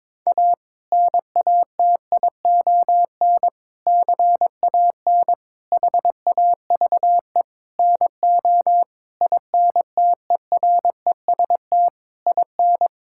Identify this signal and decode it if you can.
Morse code, heard as a keyed tone, about 22 wpm, 700 Hz. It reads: A NATION CAN HAVE NO INTEREST IN
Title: Sentences from Top 1000 Words - Review Part 6 - 22wpm